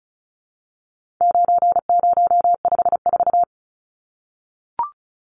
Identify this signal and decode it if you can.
Morse code, heard as a keyed tone, about 34 wpm, 700 Hz. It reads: 9054
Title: Numbers - 4-Digit 35wpm